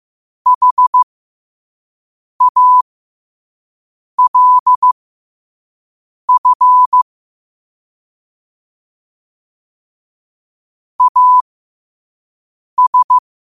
Morse code, heard as a keyed tone, about 15 wpm, spaced as for 5 wpm, 1000 Hz. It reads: HALF AS
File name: qotdfiles / 5wpm